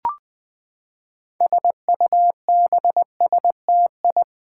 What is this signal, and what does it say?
Morse code, heard as a keyed tone, about 20 wpm, 700 Hz. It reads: SUBSTI
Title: Build A Word - 6 Letters At A Time - 12-Letter Words - 20wpm